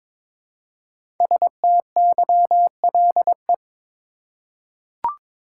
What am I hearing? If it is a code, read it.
Morse code, heard as a keyed tone, about 22 wpm, 700 Hz. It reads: STYLE
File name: 5 Letter Words - First 3 Letters Given - 22wpm